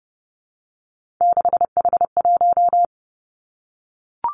Morse code, heard as a keyed tone, about 30 wpm, 700 Hz. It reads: 6H1
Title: Three Challenging Letter-Number Combinations 30wpm